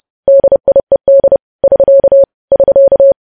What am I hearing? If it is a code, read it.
Morse code, heard as a keyed tone, about 30 wpm, 550 Hz. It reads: DIED <SK> <SK>
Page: news headlines in morse code at 30 wpm